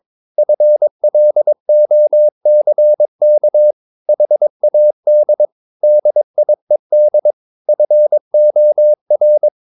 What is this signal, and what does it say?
Morse code, heard as a keyed tone, about 22 wpm, 600 Hz. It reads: FLOCK HAD DIED FOR
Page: Romanen Never af Ken Follet på engelsk.